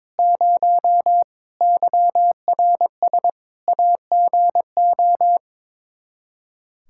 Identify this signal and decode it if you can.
Morse code, heard as a keyed tone, about 22 wpm, 700 Hz. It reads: 0 YRS AGO
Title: Practice QSO - Exchange 3 - More Info - 22wpm